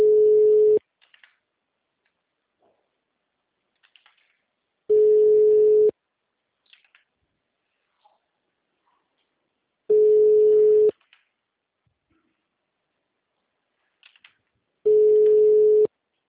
Ringback tone (Europe, 425 Hz).
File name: out- -2011- - - .77